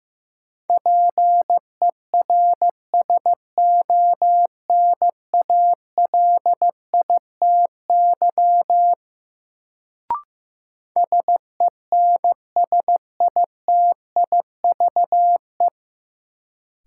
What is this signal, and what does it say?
Morse code, heard as a keyed tone, about 15 wpm, 700 Hz. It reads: PERSONALITY SENSITIVE
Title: Sentences from Top 1400 Words - New Word Drill - 15wpm